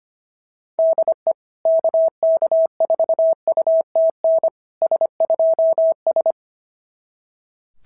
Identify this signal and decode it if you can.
Morse code, heard as a keyed tone, about 25 wpm, 690 Hz. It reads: DE KK4UTN S2S
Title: SOTA QSO Practice 25wpm